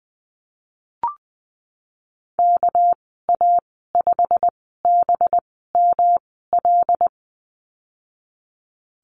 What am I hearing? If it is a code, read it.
Morse code, heard as a keyed tone, about 20 wpm, spaced as for 15 wpm, 700 Hz. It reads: KA5BML
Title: 200 Lesson 32g - Letter k - Call Signs 20wpm